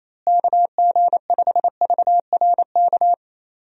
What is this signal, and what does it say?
Morse code, heard as a keyed tone, about 27 wpm, 700 Hz. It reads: KG5VRK
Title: USA Call Signs - 6 Characters - Repeated 2x - 28wpm